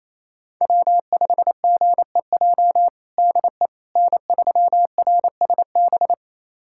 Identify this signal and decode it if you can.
Morse code, heard as a keyed tone, about 28 wpm, 700 Hz. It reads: W5GEJ DE N3RSB
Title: USA Callsign DE Callsign - 5 Characters - 28wpm